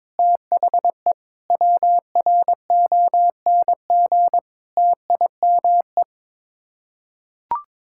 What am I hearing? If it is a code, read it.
Morse code, heard as a keyed tone, about 22 wpm, 700 Hz. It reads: THE WRONG TIME
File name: practice / 22wpm / Sets of 7 Words - 22wpm